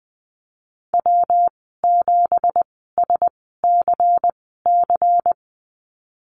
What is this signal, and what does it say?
Morse code, heard as a keyed tone, about 20 wpm, spaced as for 14 wpm, 700 Hz. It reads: W7SCC
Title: 213 Lesson 34g - Number 8 - Call Signs 20wpm